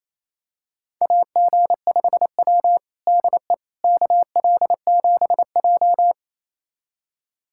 Morse code, heard as a keyed tone, about 27 wpm, 690 Hz. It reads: AG5W DE KL7J
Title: USA Callsign DE Callsign - 4 Characters - 28wpm